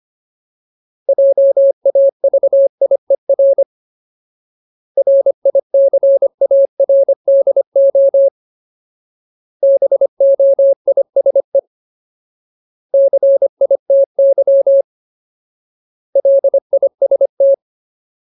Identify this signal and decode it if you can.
Morse code, heard as a keyed tone, about 25 wpm, 550 Hz. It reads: JAVIER RICARDO BOISE CITY LIST
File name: Ultimate Word List / 25WPM Ultimate